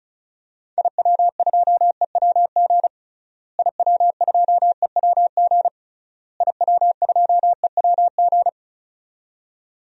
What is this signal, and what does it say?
Morse code, heard as a keyed tone, about 36 wpm, 700 Hz. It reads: IW2EWG IW2EWG IW2EWG
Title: DXCC-100 Call Signs - 6 Characters - Repeated 3x - 35wpm